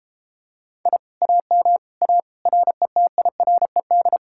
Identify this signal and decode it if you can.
Morse code, heard as a keyed tone, about 33 wpm, 700 Hz. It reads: I AM A RETIRED